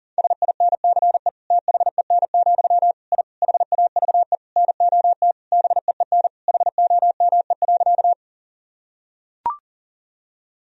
Morse code, heard as a keyed tone, about 40 wpm, 700 Hz. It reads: SINCE THEN, I HAVE NOT BEEN HOME.
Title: Sentences from Top 1000 Words - Review Part 1 - 40wpm